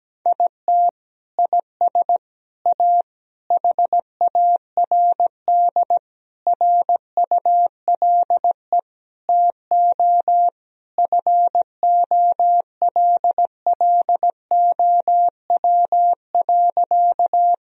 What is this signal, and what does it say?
Morse code, heard as a keyed tone, about 17 wpm, 700 Hz. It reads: IT IS A HARD RULE TO FOLLOW.
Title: Sentences from Top 1000 Words - Review Part 2 - 17wpm